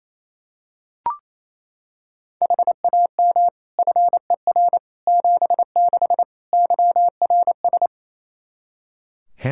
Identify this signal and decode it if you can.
Morse code, heard as a keyed tone, about 27 wpm, 700 Hz. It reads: HAM FER 76 YRS
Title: Practice QSO - Exchange 3 - More Info - 28wpm